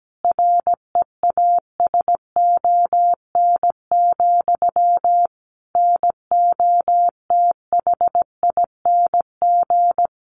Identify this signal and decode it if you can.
Morse code, heard as a keyed tone, about 17 wpm, 700 Hz. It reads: REASON, NOTHING
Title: Sentences from Top 1500 Words 17wpm